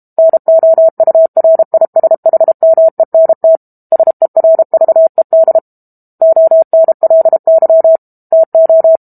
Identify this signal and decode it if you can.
Morse code, heard as a keyed tone, about 32 wpm, 650 Hz. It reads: NOURISHMENT SERVED ONLY TO
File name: Book Two / 06-Chapter 6 - 32 WPM